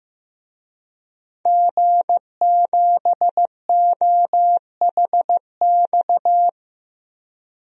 Morse code, heard as a keyed tone, about 15 wpm, 700 Hz. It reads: G7OHX
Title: DXCC-100 Call Signs - 5 Characters - 15wpm